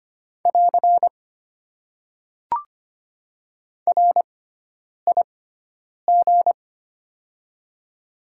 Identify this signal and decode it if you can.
Morse code, heard as a keyed tone, about 25 wpm, spaced as for 9 wpm, 700 Hz. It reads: <AR> RIG
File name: Top 90 QSO Elements - 6x Character Spacing - 25wpm